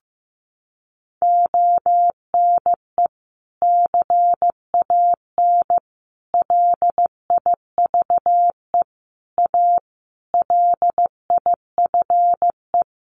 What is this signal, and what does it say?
Morse code, heard as a keyed tone, about 15 wpm, 700 Hz. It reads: ONE CAN LIVE A LIFE
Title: Sentences from Top 1400 Words 15wpm